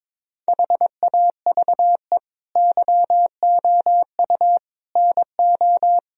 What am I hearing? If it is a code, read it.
Morse code, heard as a keyed tone, about 22 wpm, 700 Hz. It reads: HAVE YOU NO
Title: Sentences from Top 1000 Words - Review Part 1 - 22wpm